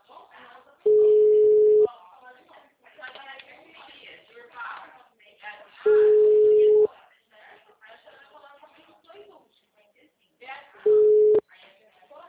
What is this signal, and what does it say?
Ringback tone (Europe, 425 Hz), with speech.